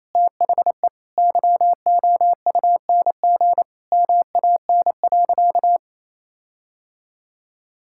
Morse code, heard as a keyed tone, about 28 wpm, 700 Hz. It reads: THE YOUNG MAN.
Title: Sentences from Top 1600 Words 28wpm